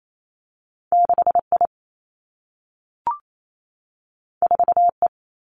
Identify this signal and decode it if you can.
Morse code, heard as a keyed tone, about 28 wpm, 700 Hz. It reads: 6 I 4 E
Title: Two Challenging Letter-Number Combinations 28wpm